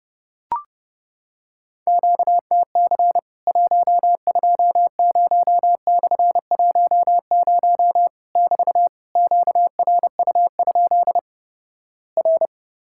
Morse code, heard as a keyed tone, about 30 wpm, 700 Hz. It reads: QTC 120/10 <BT> QRU? R
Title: Worked All Europe DX Contest - QTC Exchange - 30wpm